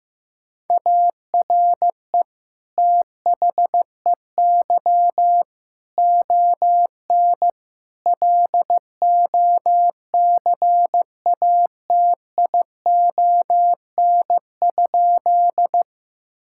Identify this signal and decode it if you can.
Morse code, heard as a keyed tone, about 15 wpm, 700 Hz. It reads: ARE THEY ON LOCATION?
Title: Sentences from Top 1000 Words - Review Part 6 - 15wpm